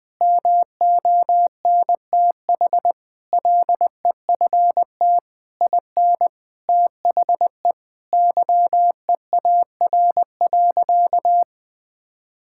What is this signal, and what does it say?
Morse code, heard as a keyed tone, about 20 wpm, 700 Hz. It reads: MONTH LEFT IN THE YEAR.
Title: Sentences from Top 1000 Words - Review Part 5 - 20wpm